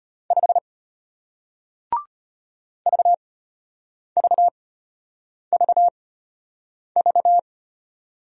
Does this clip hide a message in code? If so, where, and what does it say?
Yes: Morse code from 0.3 to 7.4 s, heard as a keyed tone, about 33 wpm, spaced as for 8 wpm, 700 Hz. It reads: 5 VVVV